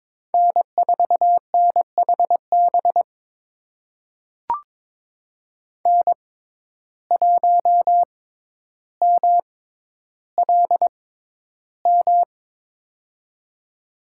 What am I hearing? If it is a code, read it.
Morse code, heard as a keyed tone, about 22 wpm, spaced as for 11 wpm, 700 Hz. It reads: N4NHB N1MLM
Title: USA Call Signs - 6x Character Spacing - 22wpm